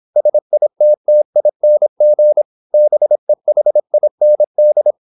Morse code, heard as a keyed tone, about 26 wpm, 600 Hz. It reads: SITTING BEHIND
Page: Never 26wpm | CW med Gnister